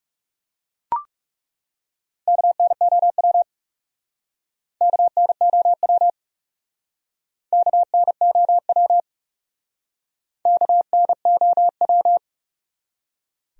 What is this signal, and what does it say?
Morse code, heard as a keyed tone, about 39 wpm, 700 Hz. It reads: KNOW KNOW KNOW KNOW